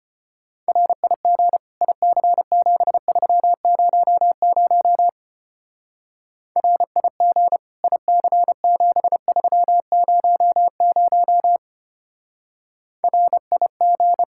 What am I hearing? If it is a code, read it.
Morse code, heard as a keyed tone, about 31 wpm, 700 Hz. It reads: RIG IC7300 RIG IC7300 RIG
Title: Speed-Racing Common QSO Phrases - 25wpm